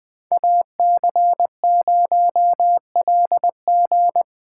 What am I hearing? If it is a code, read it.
Morse code, heard as a keyed tone, about 20 wpm, 700 Hz. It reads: AC0LG
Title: Speed Racing - USA 2x2 Call Signs 17wpm